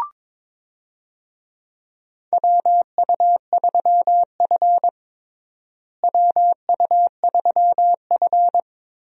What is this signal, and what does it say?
Morse code, heard as a keyed tone, about 22 wpm, 700 Hz. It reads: WU3F WU3F
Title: USA Call Signs - 4 Characters - Repeated 3x - 22wpm